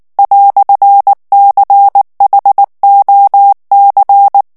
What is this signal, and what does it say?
Morse code, heard as a keyed tone, about 19 wpm, 800 Hz. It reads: "CHOC
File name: morse mp3 / FRANCAIS TEXTE CLAIR 1 VITESSE 18 MOTS MINUTES